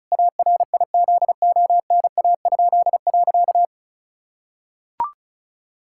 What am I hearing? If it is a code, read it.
Morse code, heard as a keyed tone, about 36 wpm, 700 Hz. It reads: ARIZONA?.
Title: Pangram Challenge v2 - 35wpm